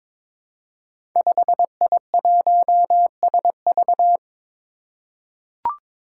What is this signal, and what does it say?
Morse code, heard as a keyed tone, about 23 wpm, 700 Hz. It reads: 5I1SV
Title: Five Challenging Letter-Number Combinations 22wpm